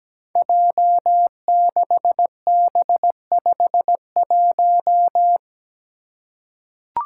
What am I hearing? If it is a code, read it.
Morse code, heard as a keyed tone, about 17 wpm, 700 Hz. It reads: J6B51
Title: Five Challenging Letter-Number Combinations 17wpm